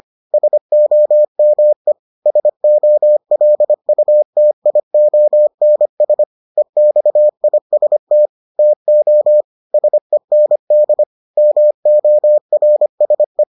Morse code, heard as a keyed tone, about 25 wpm, 600 Hz. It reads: SOME SOLUTIONS EXIST TO SEND MORSE